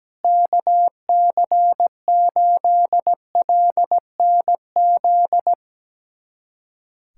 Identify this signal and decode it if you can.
Morse code, heard as a keyed tone, about 17 wpm, 700 Hz. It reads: KC8LNZ
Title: USA Callsign DE Callsign - 6 Characters - 17wpm